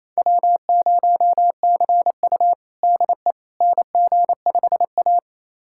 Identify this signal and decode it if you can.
Morse code, heard as a keyed tone, about 28 wpm, 700 Hz. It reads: W0CU DE NG5A
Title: USA Callsign DE Callsign - 4 Characters - 28wpm